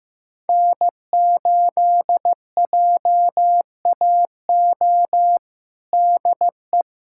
Morse code, heard as a keyed tone, about 15 wpm, 700 Hz. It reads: N8JAO DE
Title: USA Callsign DE Callsign 15wpm